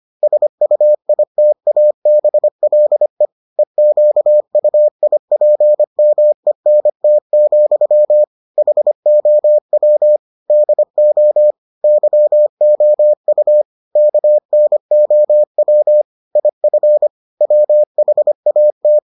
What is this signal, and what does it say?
Morse code, heard as a keyed tone, about 25 wpm, 600 Hz. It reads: SUITABLE EQUIPMENT, HOW DO YOU KNOW IF WHAT